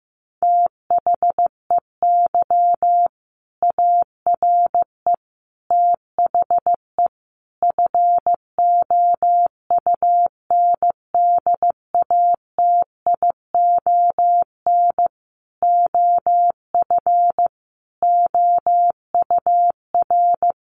Morse code, heard as a keyed tone, about 15 wpm, 700 Hz. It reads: THEY ARE THE FOUNDATION OF OUR
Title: Sentences from Top 1200 Words 15wpm